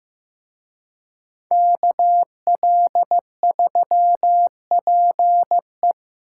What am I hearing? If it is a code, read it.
Morse code, heard as a keyed tone, about 18 wpm, 700 Hz. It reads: KL3PE
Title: Speed Racing - USA 2x2 Call Signs 15wpm